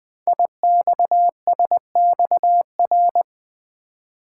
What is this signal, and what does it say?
Morse code, heard as a keyed tone, about 20 wpm, 700 Hz. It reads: IXSXR
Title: Code Groups - 5 Letter:Numbers - 20wpm